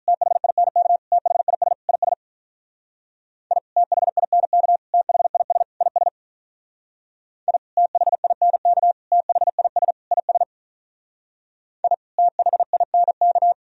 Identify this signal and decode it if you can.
Morse code, heard as a keyed tone, about 44 wpm, 700 Hz. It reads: THINK THIS IS I THINK THIS IS I THINK THIS IS I THINK
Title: Speed-Racing - Sets of 4 Words 35wpm